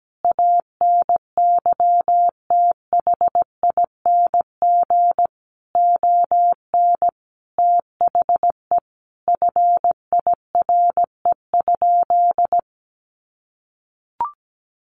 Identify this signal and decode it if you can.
Morse code, heard as a keyed tone, about 17 wpm, 700 Hz. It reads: ANYTHING ON THE FIRE?
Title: Sentences from Top 1000 Words 17wpm